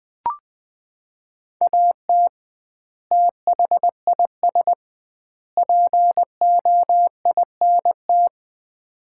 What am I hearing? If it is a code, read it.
Morse code, heard as a keyed tone, about 21 wpm, 700 Hz. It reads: AT THIS POINT
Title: 2x Word-Spacing - Sets of 3 Words - 20wpm